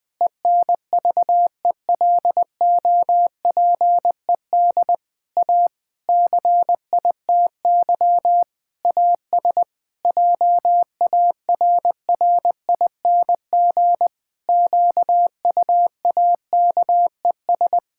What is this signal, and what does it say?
Morse code, heard as a keyed tone, about 21 wpm, 700 Hz. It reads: ENVELOPED A CITY AS JARRING QUAKES
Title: Pangram Challenge v2 - 20wpm